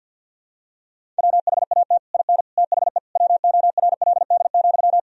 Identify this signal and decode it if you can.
Morse code, heard as a keyed tone, about 51 wpm, 700 Hz. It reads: WHAT IN THE WORLD,
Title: Sentences from Top 1000 Words - Review Part 1 - 50wpm